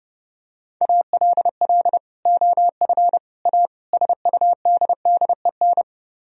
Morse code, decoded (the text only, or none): ALL OF A SUDDEN